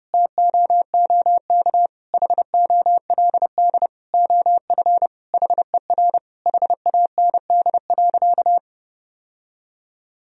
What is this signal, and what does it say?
Morse code, heard as a keyed tone, about 30 wpm, 700 Hz. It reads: TOOK HOLD OF HER HAND.
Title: Sentences from Top 1000 Words - Review Part 2 - 30wpm